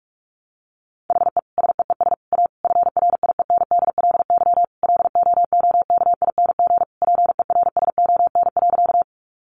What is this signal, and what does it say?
Morse code, heard as a keyed tone, about 44 wpm, 700 Hz. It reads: HE SEES A FRIENDLY LOOKING PERSON.
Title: Easy Phrases 45wpm